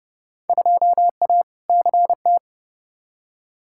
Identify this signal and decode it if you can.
Morse code, heard as a keyed tone, about 29 wpm, 700 Hz. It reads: 2A CT
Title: Field Day Exchanges - 30wpm